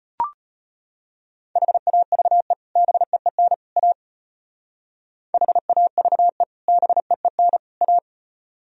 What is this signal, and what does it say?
Morse code, heard as a keyed tone, about 32 wpm, 700 Hz. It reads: HAVE BEEN A HAVE BEEN A